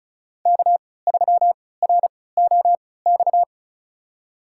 Morse code, heard as a keyed tone, about 36 wpm, spaced as for 20 wpm, 690 Hz. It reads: K3ROX
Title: USA Call Signs - 5 Characters - 3x Character Spacing - 35wpm